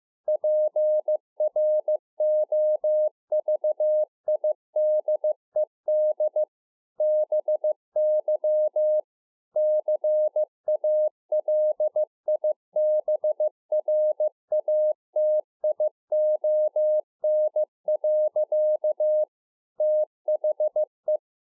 Morse code, heard as a keyed tone, about 15 wpm, 610 Hz. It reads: PROVIDED BY CALIBRATION. THE
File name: 2011-04-mp3 / A SMALL TRANCEIVER 2 15WPM